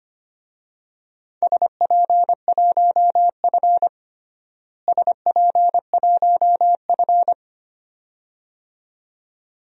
Morse code, heard as a keyed tone, about 25 wpm, 700 Hz. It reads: SP1F SP1F
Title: DXCC-100 Call Signs - 4 Characters - Repeated 2x - 25wpm